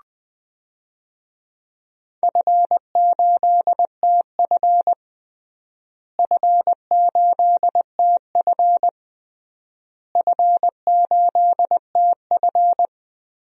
Morse code, heard as a keyed tone, about 20 wpm, 700 Hz. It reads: F8TF F8TF F8TF
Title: DXCC-100 Call Signs - 4 Characters - Repeated 3x - 20wpm